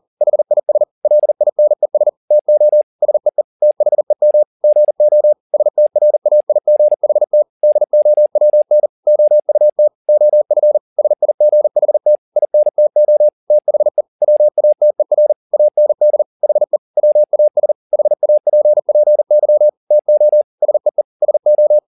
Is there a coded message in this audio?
Yes: Morse code from 0.2 to 21.8 s, heard as a keyed tone, about 40 wpm, 600 Hz. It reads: HIS LINES TO SEE THEM GO STRAIGHT DOWN OUT OF SIGHT INTO THE WATER AND HE WAS HAPPY TO SEE SO